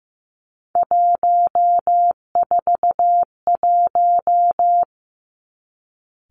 Morse code, heard as a keyed tone, about 15 wpm, 700 Hz. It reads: 141
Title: Three Challenging Letter-Number Combinations 15wpm